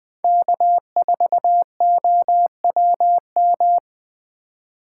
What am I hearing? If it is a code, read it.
Morse code, heard as a keyed tone, about 21 wpm, 700 Hz. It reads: K4OWM
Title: Speed Racing - USA 1x3 Call Signs 17wpm